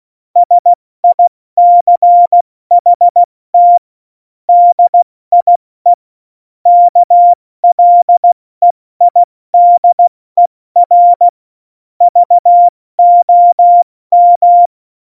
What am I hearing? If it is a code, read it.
Morse code, heard as a keyed tone, about 16 wpm, 700 Hz. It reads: SICHT DIE KLEIDER VOM
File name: Morsekurs / Witze / Witz 17. November 2020, 70 F80